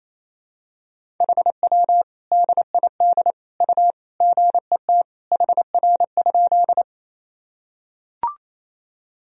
Morse code, heard as a keyed tone, about 28 wpm, 700 Hz. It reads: HW DID U GET HR?
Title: Sentences from Top 200 Words and Abbreviations 28wpm